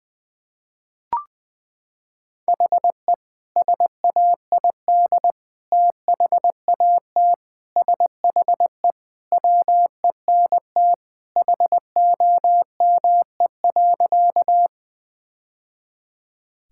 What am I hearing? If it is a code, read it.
Morse code, heard as a keyed tone, about 21 wpm, 700 Hz. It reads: HE SAID THAT SHE WENT HOME.
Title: Sentences with 2-Word Phrases - 20wpm